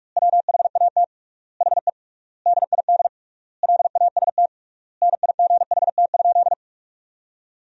Morse code, heard as a keyed tone, about 47 wpm, 700 Hz. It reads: WHAT HE DID LAST NIGHT?
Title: 3x Word-Spacing - Sentences from Top 300 Words - 45wpm